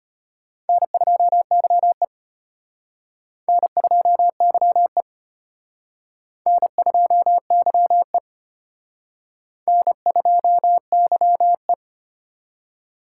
Morse code, heard as a keyed tone, about 30 wpm, 700 Hz. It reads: N2YE N2YE N2YE N2YE
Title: Speed Racing - USA Call Signs 25wpm